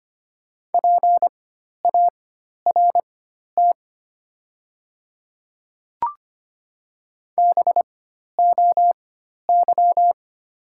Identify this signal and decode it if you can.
Morse code, heard as a keyed tone, about 25 wpm, spaced as for 12 wpm, 700 Hz. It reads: PART BOY
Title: 4x Character Spacing - Top 500 Words - 25wpm